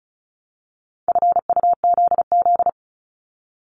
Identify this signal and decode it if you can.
Morse code, heard as a keyed tone, about 36 wpm, 700 Hz. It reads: FUZZ
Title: Multi-letter Phonemes Repeated 3x 35wpm